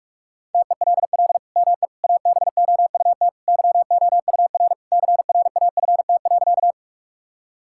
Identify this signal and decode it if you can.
Morse code, heard as a keyed tone, about 45 wpm, 700 Hz. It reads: TELL ME ABOUT YOUR CRAFT.
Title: Sentences from Top 1000 Words - Review Part 6 - 45wpm